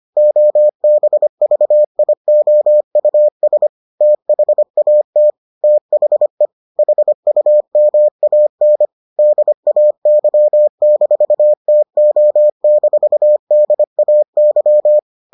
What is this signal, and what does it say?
Morse code, heard as a keyed tone, about 25 wpm, 600 Hz. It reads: OBVIOUS THAT THE HUMAN DAY-TO-DAY